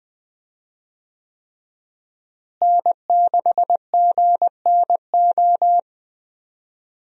Morse code, heard as a keyed tone, about 20 wpm, 700 Hz. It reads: N6GNO
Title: USA Call Signs - 5 Characters - Repeated 3x - 20wpm